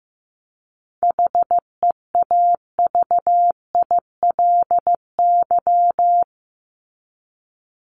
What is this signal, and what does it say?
Morse code, heard as a keyed tone, about 15 wpm, 700 Hz. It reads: HEAVILY
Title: LY Suffix - Top 100 Words with Base Words 15wpm